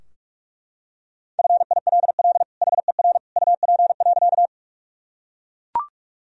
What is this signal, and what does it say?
Morse code, heard as a keyed tone, about 42 wpm, 700 Hz. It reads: FILL HER UP.
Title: Sentences from Top 1000 Words - Review Part 3 - 45wpm